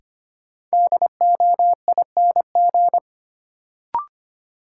Morse code, heard as a keyed tone, about 25 wpm, 700 Hz. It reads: DOING
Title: Sets of 4 Words - New Word Drill - 25wpm